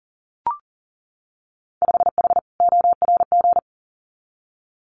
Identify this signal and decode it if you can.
Morse code, heard as a keyed tone, about 39 wpm, 700 Hz. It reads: 5H ORG
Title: Winter Field Day Exchanges Encore 40wpm